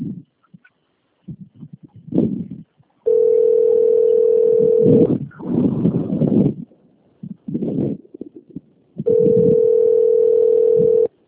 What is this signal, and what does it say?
Ringback tone (North America), 440 + 480 Hz.